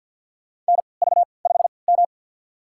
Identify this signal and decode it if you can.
Morse code, heard as a keyed tone, about 53 wpm, spaced as for 29 wpm, 700 Hz. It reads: NV5M
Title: USA Call Signs - 4 Characters - 3x Character Spacing - 50wpm